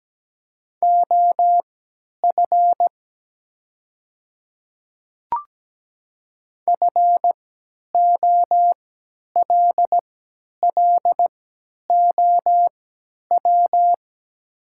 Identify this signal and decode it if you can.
Morse code, heard as a keyed tone, about 17 wpm, spaced as for 10 wpm, 700 Hz. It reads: OF FOLLOW
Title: 3x Character Spacing - Top 200 Words - 17wpm